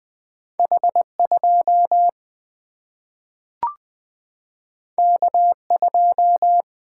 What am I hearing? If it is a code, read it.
Morse code, heard as a keyed tone, about 20 wpm, 700 Hz. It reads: H 2 K 2
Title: Single-Letter Single-Number 20wpm